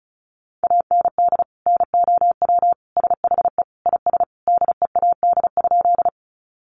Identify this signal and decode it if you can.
Morse code, heard as a keyed tone, about 34 wpm, 700 Hz. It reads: AND NOW SHE IS DEAD?
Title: Sentences from Top 1500 Words 35wpm